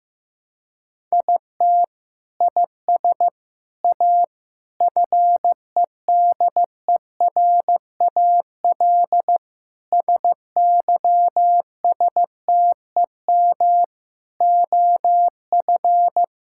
Morse code, heard as a keyed tone, about 15 wpm, 700 Hz. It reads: IT IS A FEDERAL SYSTEM OF
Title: Sentences from Top 1000 Words - Review Part 5 - 15wpm